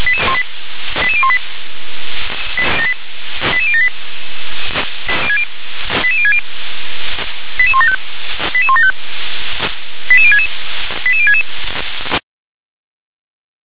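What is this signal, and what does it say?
Five-tone selective call (ZVEI), heard as quick runs of five tones, tones of about 70 ms. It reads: 88018 88078 88060 88175 88175 88066 88066
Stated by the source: ZVEI five-tone selective call received via SDR.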